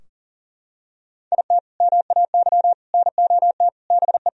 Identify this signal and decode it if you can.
Morse code, heard as a keyed tone, about 40 wpm, 700 Hz. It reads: IT MAY NOT BE
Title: Sentences from Top 1000 Words - Review Part 4 - 40wpm